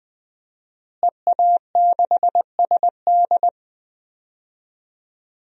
Morse code, heard as a keyed tone, about 20 wpm, 700 Hz. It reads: EA6SD
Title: DXCC-100 Call Signs - 5 Characters - Repeated 3x - 20wpm